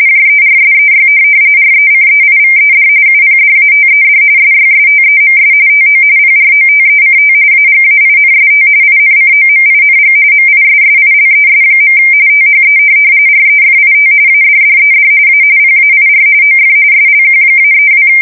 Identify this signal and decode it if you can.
RTTY, about 45.45 baud. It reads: JERSEY RECEIVED UPDATED SYSTEMS TO PROTECT AGAINST INCOMING MISSILES. THE REFIT ADDED A PAIR OF SLQ-32
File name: fsk170 / USS-New-Jersey